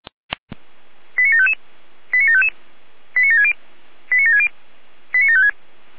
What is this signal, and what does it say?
Five-tone selective call (ZVEI), heard as quick runs of five tones, tones of about 80 ms. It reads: 89755 89755 89760 89760 89756